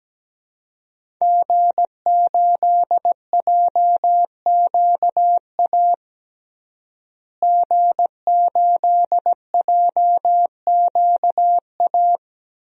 Morse code, heard as a keyed tone, about 17 wpm, 700 Hz. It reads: G8JQA G8JQA
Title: DXCC-100 Call Signs - 5 Characters - Repeated 3x - 17wpm